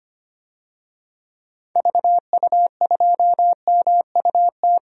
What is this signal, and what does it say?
Morse code, heard as a keyed tone, about 25 wpm, 700 Hz. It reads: VU2MUT
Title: DXCC-100 Call Signs - 6 Characters - Repeated 2x - 25wpm